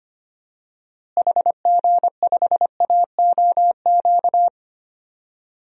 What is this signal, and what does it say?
Morse code, heard as a keyed tone, about 25 wpm, 700 Hz. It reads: HG5AOQ
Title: DXCC-100 Call Signs - 6 Characters - Repeated 3x - 25wpm